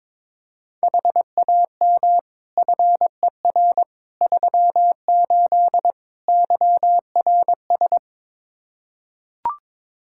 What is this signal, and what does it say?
Morse code, heard as a keyed tone, about 22 wpm, 700 Hz. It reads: HAM FER 38 YRS
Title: K1USN SST QSO Practice 22wpm